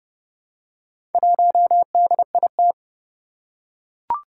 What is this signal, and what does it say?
Morse code, heard as a keyed tone, about 30 wpm, 700 Hz. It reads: 1DIT
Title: Four Challenging Letter-Number Combinations 30wpm